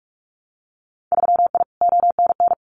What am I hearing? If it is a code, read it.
Morse code, heard as a keyed tone, about 44 wpm, 700 Hz. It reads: 3I ONN
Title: Winter Field Day Exchanges Encore 45wpm